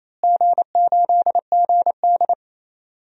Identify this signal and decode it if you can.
Morse code, heard as a keyed tone, about 28 wpm, 700 Hz. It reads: G8GD
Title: DXCC-100 Call Signs - 4 Characters - Repeated 3x - 28wpm